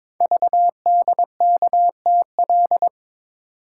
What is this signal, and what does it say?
Morse code, heard as a keyed tone, about 22 wpm, 700 Hz. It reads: VDKTL
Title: Code Groups - 5 Letters - 22wpm